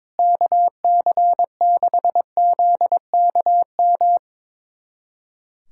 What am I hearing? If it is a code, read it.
Morse code, heard as a keyed tone, about 22 wpm, 700 Hz. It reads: KC6ZKM
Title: USA Callsign DE Callsign - 6 Characters - 22wpm